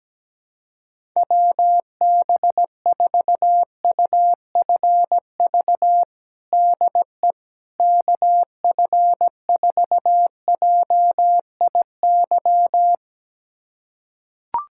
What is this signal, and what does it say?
Morse code, heard as a keyed tone, about 17 wpm, 700 Hz. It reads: WB4UFV DE KF4JIY
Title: USA Callsign DE Callsign - 6 Characters - 17wpm